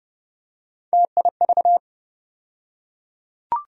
Morse code, heard as a keyed tone, about 30 wpm, 700 Hz. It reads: TIV
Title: USA Call Sign Postfixes - 3 Letters 30wpm